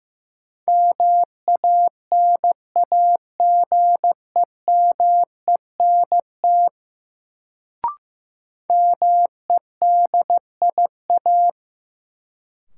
Morse code, heard as a keyed tone, about 15 wpm, 700 Hz. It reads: MANAGEMENT MEDIA
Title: Sentences from Top 1000 Words - Review Part 4 - New Word Drill - 15wpm